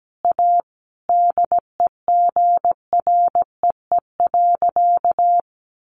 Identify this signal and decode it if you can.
Morse code, heard as a keyed tone, about 17 wpm, 700 Hz. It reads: A DEGREE.
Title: Sentences from Top 1500 Words 17wpm